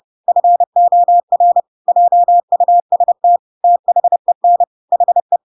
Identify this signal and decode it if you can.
Morse code, heard as a keyed tone, about 30 wpm, 700 Hz. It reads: FOR JUST THEN HE
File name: Pinocchio / Pinocchio ch4 30 30 ik1hge